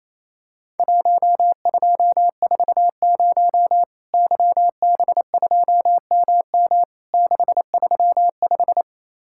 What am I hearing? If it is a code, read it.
Morse code, heard as a keyed tone, about 28 wpm, 700 Hz. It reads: 1240 YB2MM 635
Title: Worked All Europe DX Contest - QTC Exchange - 28wpm